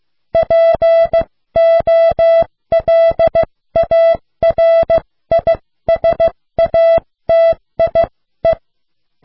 Morse code, heard as a keyed tone, about 15 wpm, spaced as for 13 wpm, 660 Hz. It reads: POLARISATIE